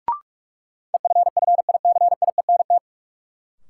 Morse code, heard as a keyed tone, about 44 wpm, 700 Hz. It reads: EFFICIENT
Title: Sentences from Top 1300 Words - New Word Drill - 45wpm